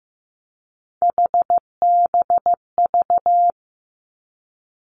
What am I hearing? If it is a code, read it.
Morse code, heard as a keyed tone, about 15 wpm, 700 Hz. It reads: HBV
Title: Three Challenging Letter-Number Combinations 15wpm